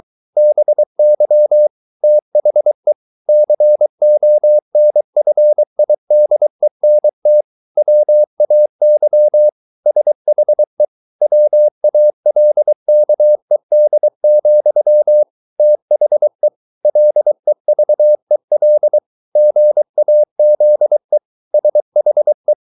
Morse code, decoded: BY THE CONFIDENT WAY SHE WALKED, THE LEVEL GAZE SHE

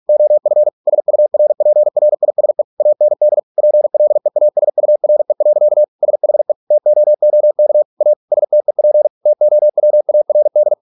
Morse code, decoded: OF SURPRISE AND PLEASURE. SHE TOOK A STEP TOWARD

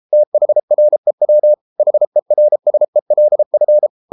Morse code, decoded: THREW HERSELF

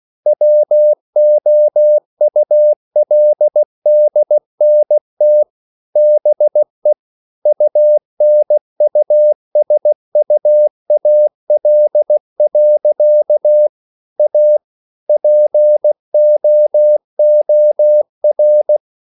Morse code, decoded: WOULDNT BE UNUSUAL. A POOR

